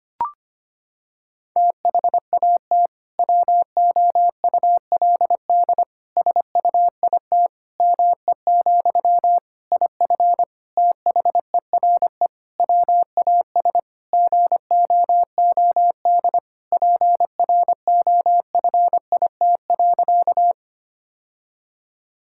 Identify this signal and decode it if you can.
Morse code, heard as a keyed tone, about 25 wpm, 700 Hz. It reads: THAT WOULD SUIT ME, IF THERE WAS GOOD PROFIT.
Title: Sentences from Top 1600 Words 25wpm